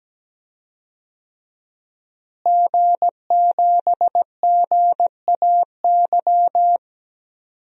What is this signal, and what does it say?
Morse code, heard as a keyed tone, about 17 wpm, 700 Hz. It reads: G7GAY
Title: DXCC-100 Call Signs - 5 Characters - Repeated 3x - 17wpm